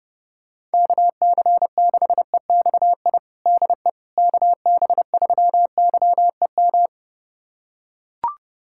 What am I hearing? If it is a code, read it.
Morse code, heard as a keyed tone, about 32 wpm, 700 Hz. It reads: KC6EXI DE KB3YEM
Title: USA Callsign DE Callsign 30wpm